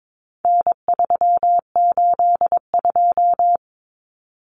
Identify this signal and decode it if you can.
Morse code, heard as a keyed tone, about 22 wpm, 700 Hz. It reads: N382
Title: Common Cut Numbers - 4-Digit 22wpm